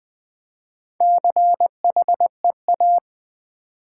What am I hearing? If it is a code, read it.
Morse code, heard as a keyed tone, about 20 wpm, 700 Hz. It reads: CHEA
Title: Build A Word - 1 Letter At A Time - 5-Letter Words - 20wpm